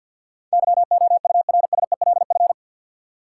Morse code, heard as a keyed tone, about 52 wpm, 700 Hz. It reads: YOURSELF
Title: Sentences from Top 600 Words - New Word Drill - 50wpm